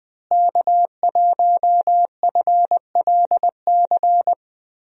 Morse code, heard as a keyed tone, about 20 wpm, 700 Hz. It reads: K1FLC
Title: USA Call Signs - 5 Characters - Repeated 2x - 20wpm